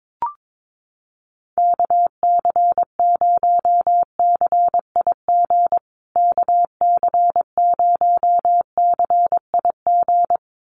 Morse code, heard as a keyed tone, about 22 wpm, 700 Hz. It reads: KC0CIG KC0CIG
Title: USA Call Signs Repeated 2x 22wpm